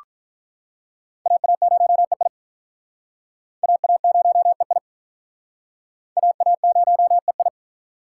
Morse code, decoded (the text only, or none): AA0EI AA0EI AA0EI